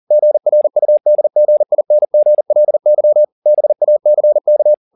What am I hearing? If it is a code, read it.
Morse code, heard as a keyed tone, about 40 wpm, 600 Hz. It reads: GRUDGINGLY BACK
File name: the old man and the sea 2 40wpm